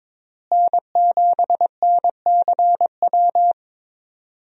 Morse code, heard as a keyed tone, about 22 wpm, 700 Hz. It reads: N7NCW
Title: USA Call Signs - 5 Characters - Repeated 2x - 22wpm